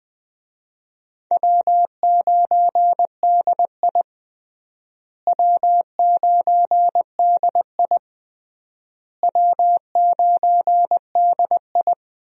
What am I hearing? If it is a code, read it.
Morse code, heard as a keyed tone, about 21 wpm, 700 Hz. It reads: W9DI W9DI W9DI
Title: USA Call Signs - 4 Characters - Repeated 3x - 20wpm